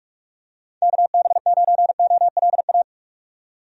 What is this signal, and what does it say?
Morse code, heard as a keyed tone, about 44 wpm, 700 Hz. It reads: KB9OLA
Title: USA Call Signs - 6 Characters - 45wpm